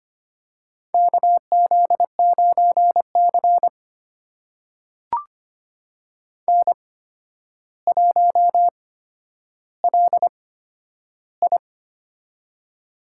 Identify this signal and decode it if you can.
Morse code, heard as a keyed tone, about 25 wpm, spaced as for 11 wpm, 700 Hz. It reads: KZ9C N1LI
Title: USA Call Signs - 4 Characters - 8x Character Spacing - 25wpm